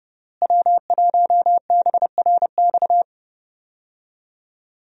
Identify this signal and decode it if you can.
Morse code, heard as a keyed tone, about 30 wpm, 700 Hz. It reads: W1BRX
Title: USA Call Signs - 5 Characters - Repeated 3x - 30wpm